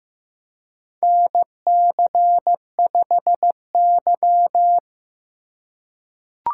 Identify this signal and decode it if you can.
Morse code, heard as a keyed tone, about 15 wpm, 700 Hz. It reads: NC5Y
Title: USA Call Signs - 4 Characters - 4x Character Spacing - 15wpm